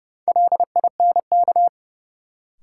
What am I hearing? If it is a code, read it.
Morse code, heard as a keyed tone, about 30 wpm, 700 Hz. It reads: LINK
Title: Sentences from Top 1500 Words - New Word Drill Alt - 30wpm